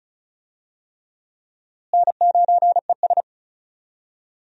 Morse code, heard as a keyed tone, about 36 wpm, 700 Hz. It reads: N9ES
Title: USA Call Signs - 4 Characters - Repeated 3x - 35wpm